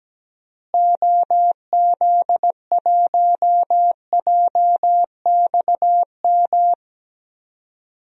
Morse code, heard as a keyed tone, about 17 wpm, 700 Hz. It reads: OZ1JXM
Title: DXCC-100 Call Signs - 6 Characters - Repeated 2x - 17wpm